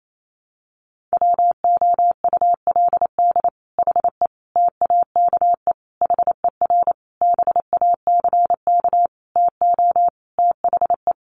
Morse code, heard as a keyed tone, about 28 wpm, 700 Hz. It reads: WOULD HE TAKE HER BACK TO THE
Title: Sentences from Top 1500 Words 28wpm